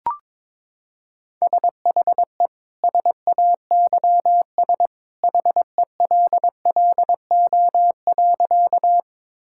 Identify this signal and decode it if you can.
Morse code, heard as a keyed tone, about 22 wpm, 700 Hz. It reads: SHE SAYS HELLO.
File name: practice / 22wpm / Sentences with 2-Word Phrases - 22wpm